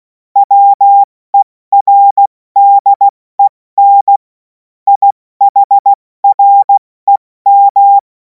Morse code, heard as a keyed tone, about 16 wpm, 800 Hz. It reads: WERDEN IHREM
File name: Morsekurs / Witze / Witz 24. März 2026 70 F80